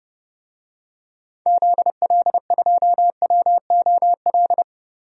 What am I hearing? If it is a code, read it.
Morse code, heard as a keyed tone, about 31 wpm, 700 Hz. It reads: ZL2WOL
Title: DXCC-100 Call Signs - 6 Characters - 30wpm